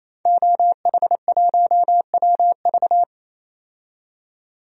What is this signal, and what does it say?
Morse code, heard as a keyed tone, about 28 wpm, 700 Hz. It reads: OH1WV
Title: DXCC-100 Call Signs - 5 Characters - 28wpm